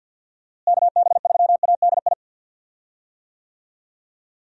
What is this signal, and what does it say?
Morse code, heard as a keyed tone, about 51 wpm, 700 Hz. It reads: KB3ADI
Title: USA Call Signs - 6 Characters - Repeated 2x - 50wpm